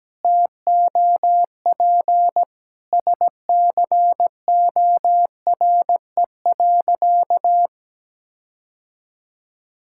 Morse code, heard as a keyed tone, about 17 wpm, 700 Hz. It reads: TOP SCORE.
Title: Sentences from Top 1600 Words 17wpm